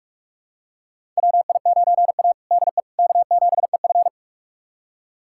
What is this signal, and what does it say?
Morse code, heard as a keyed tone, about 47 wpm, 700 Hz. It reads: WI9A DE K7EF
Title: USA Callsign DE Callsign 45wpm